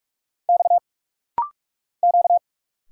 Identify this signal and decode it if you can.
Morse code, heard as a keyed tone, about 46 wpm, 700 Hz. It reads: X Q
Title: Single Letters - Rapid-Fire 45wpm